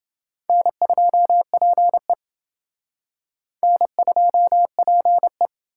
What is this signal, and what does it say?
Morse code, heard as a keyed tone, about 29 wpm, 700 Hz. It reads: N2PE N2PE